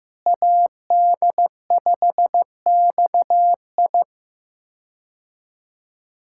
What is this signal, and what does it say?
Morse code, heard as a keyed tone, about 16 wpm, 700 Hz. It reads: AD5XI
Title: K1USN SST QSO Practice - 15wpm